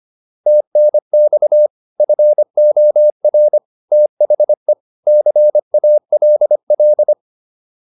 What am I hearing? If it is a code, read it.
Morse code, heard as a keyed tone, about 25 wpm, 600 Hz. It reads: TNX FOR THE CALL
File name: PR302 25